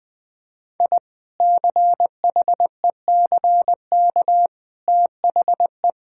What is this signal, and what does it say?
Morse code, heard as a keyed tone, about 20 wpm, 700 Hz. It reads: I CHECK THE
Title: Difficult Sentences with Hints - 20wpm